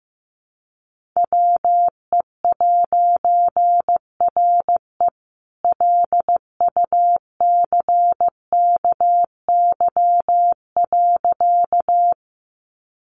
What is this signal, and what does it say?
Morse code, heard as a keyed tone, about 15 wpm, 700 Hz. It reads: WE'RE LUCKY.
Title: A Sound Of Thunder 15wpm